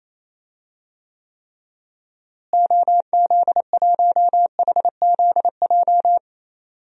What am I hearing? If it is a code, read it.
Morse code, heard as a keyed tone, about 28 wpm, 700 Hz. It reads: OZ1HZJ
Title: DXCC-100 Call Signs - 6 Characters - Repeated 3x - 28wpm